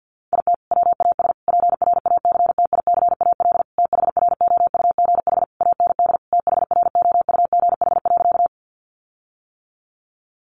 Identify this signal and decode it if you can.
Morse code, heard as a keyed tone, about 50 wpm, 700 Hz. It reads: IT WAS PRACTICAL THROUGH AND THROUGH.
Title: Sentences from Top 1400 Words 50wpm